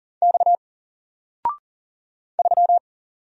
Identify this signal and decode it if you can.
Morse code, heard as a keyed tone, about 40 wpm, 700 Hz. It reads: X 3
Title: Single Letter-Number 40wpm